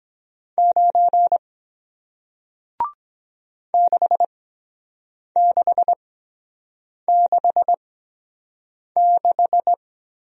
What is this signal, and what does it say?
Morse code, heard as a keyed tone, about 23 wpm, spaced as for 7 wpm, 700 Hz. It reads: 9 6666